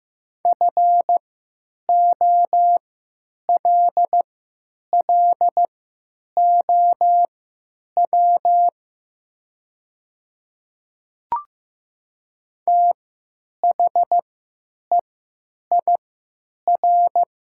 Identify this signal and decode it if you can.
Morse code, heard as a keyed tone, about 15 wpm, spaced as for 9 wpm, 700 Hz. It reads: FOLLOW THEIR
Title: 3x Character Spacing - Top 300 Words - 15wpm